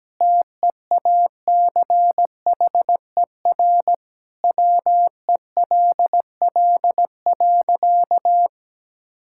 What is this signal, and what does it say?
Morse code, heard as a keyed tone, about 17 wpm, 700 Hz. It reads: TEACHER WELL.
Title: Sentences from Top 1000 Words - Review Part 6 - 17wpm